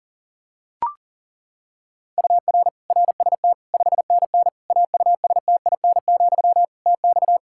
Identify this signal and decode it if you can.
Morse code, heard as a keyed tone, about 40 wpm, 700 Hz. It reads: UR RST 5NN AUSTIN, TX
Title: Practice QSO - Exchange 1 - Beginning - 40wpm